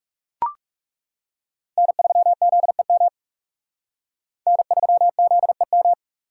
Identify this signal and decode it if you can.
Morse code, heard as a keyed tone, about 37 wpm, 700 Hz. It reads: N3ZEM N3ZEM
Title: Speed Racing - USA 1x3 Call Signs 30wpm